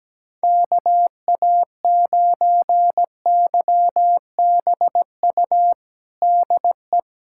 Morse code, heard as a keyed tone, about 17 wpm, 700 Hz. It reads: KA9YBU DE
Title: USA Callsign DE Callsign 17wpm